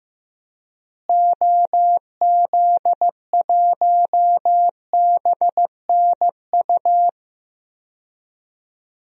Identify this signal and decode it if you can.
Morse code, heard as a keyed tone, about 15 wpm, 700 Hz. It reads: OZ1BNU
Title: DXCC-100 Call Signs - 6 Characters - Repeated 2x - 15wpm